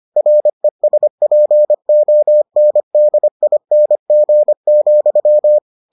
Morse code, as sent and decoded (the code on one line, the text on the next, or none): .-. ... .--. --- -. -.. .. -. --. --..--
RESPONDING,